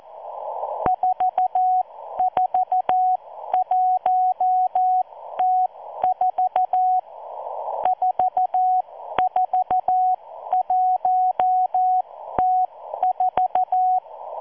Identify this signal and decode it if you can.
Morse code, heard as a keyed tone, about 14 wpm, spaced as for 12 wpm, 740 Hz. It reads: 441T4 441T4